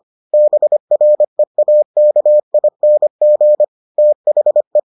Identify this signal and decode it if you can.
Morse code, heard as a keyed tone, about 25 wpm, 600 Hz. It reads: BREAKING THE